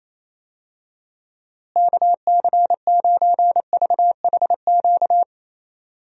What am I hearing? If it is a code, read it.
Morse code, heard as a keyed tone, about 28 wpm, 700 Hz. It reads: KC9VHQ
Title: USA Call Signs - 6 Characters - Repeated 2x - 28wpm